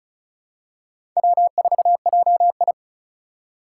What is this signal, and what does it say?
Morse code, heard as a keyed tone, about 36 wpm, 690 Hz. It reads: W4JI
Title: USA Call Signs - 4 Characters - 2x Character Spacing - 35wpm